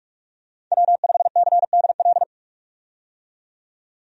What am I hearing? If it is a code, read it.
Morse code, heard as a keyed tone, about 45 wpm, 700 Hz. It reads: W5CDL
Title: USA Call Signs - 5 Characters - 45wpm